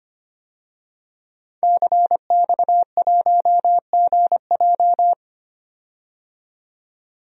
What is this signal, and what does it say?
Morse code, heard as a keyed tone, about 25 wpm, 700 Hz. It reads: CX1GJ
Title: DXCC-100 Call Signs - 5 Characters - 25wpm